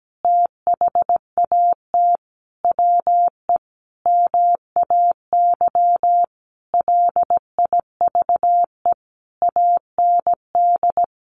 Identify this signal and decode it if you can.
Morse code, heard as a keyed tone, about 17 wpm, 700 Hz. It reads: THAT WE MAY LIVE AND
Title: Sentences from Top 800 Words 17wpm